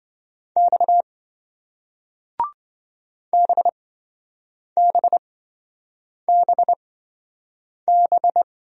Morse code, heard as a keyed tone, about 28 wpm, spaced as for 7 wpm, 700 Hz. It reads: X BBBB